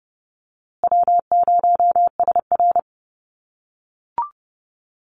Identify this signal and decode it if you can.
Morse code, heard as a keyed tone, about 30 wpm, 700 Hz. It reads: W0SR
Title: USA Call Signs Repeated 3x 30wpm